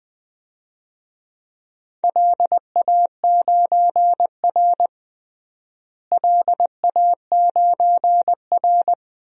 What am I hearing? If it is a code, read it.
Morse code, heard as a keyed tone, about 20 wpm, 700 Hz. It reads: LA9R LA9R
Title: DXCC-100 Call Signs - 4 Characters - Repeated 2x - 20wpm